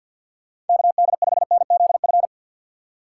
Morse code, heard as a keyed tone, about 48 wpm, 700 Hz. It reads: KD5NZF
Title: USA Call Signs - 2x Character Spacing - 50wpm